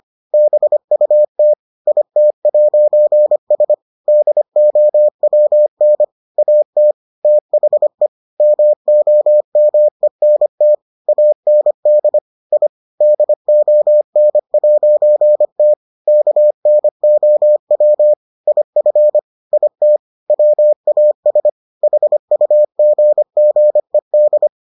Morse code, heard as a keyed tone, about 25 wpm, 600 Hz. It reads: BUT IT'S DOWN AT THE MOMENT AND I DON'T KNOW IF IT WAS HUGGED